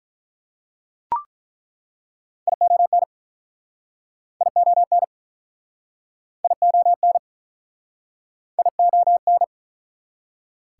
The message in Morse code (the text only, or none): ION ION ION ION